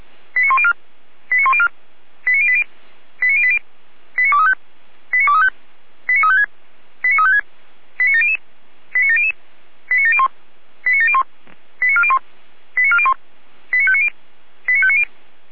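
Five-tone selective call (ZVEI), heard as quick runs of five tones, tones of about 70 ms. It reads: 89104 89104 89080 89080 89235 89235 89356 89356 89700 89700 89701 89701 89401 89401 89590 89590
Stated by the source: ZVEI five-tone call 89020 received by SDR.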